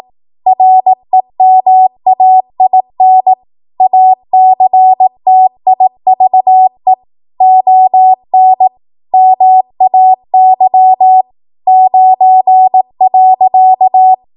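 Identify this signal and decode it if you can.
Morse code, heard as a keyed tone, about 18 wpm, 750 Hz. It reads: REMAIN ACTIVE ON MAY 9.
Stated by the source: 18 WPM Morse code practice from a national amateur radio association's code archive.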